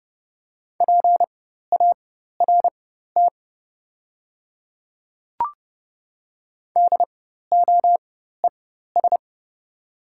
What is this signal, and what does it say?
Morse code, heard as a keyed tone, about 30 wpm, spaced as for 14 wpm, 690 Hz. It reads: PART DOES
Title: 4x Character Spacing - Top 200 Words - 30wpm